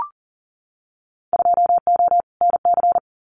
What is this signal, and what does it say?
Morse code, heard as a keyed tone, about 40 wpm, 700 Hz. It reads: 2O NC
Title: Winter Field Day Exchanges 40wpm